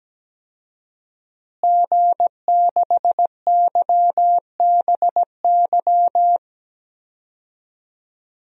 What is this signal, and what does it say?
Morse code, heard as a keyed tone, about 17 wpm, 700 Hz. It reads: G6YBY
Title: DXCC-100 Call Signs - 5 Characters - Repeated 2x - 17wpm